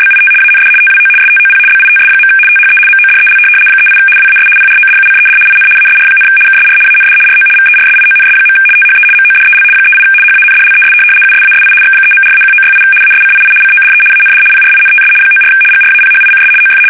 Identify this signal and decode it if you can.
RTTY, about 75 baud. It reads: BIT AND TWO STOP BITS. THE MODEL 38 WAS PRODUCED IN ASR, KSR AND RO VERSIONS. THE MODEL 38 HANDLES USASI X3.4-1968 ASCII CODE AND PRINTS 94 GRAPHICS OF
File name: fsk850-75B / teletype-corp